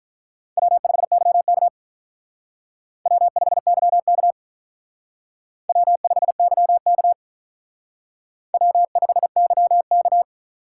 Morse code, heard as a keyed tone, about 44 wpm, 700 Hz. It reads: W5YK W5YK W5YK W5YK